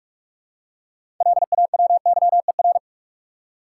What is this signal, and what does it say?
Morse code, heard as a keyed tone, about 46 wpm, 700 Hz. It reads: LAWYER
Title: Six Letter Words - 45wpm